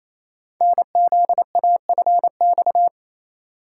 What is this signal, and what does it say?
Morse code, heard as a keyed tone, about 27 wpm, 700 Hz. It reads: NZAFX
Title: Code Groups - 5 Letters - 28wpm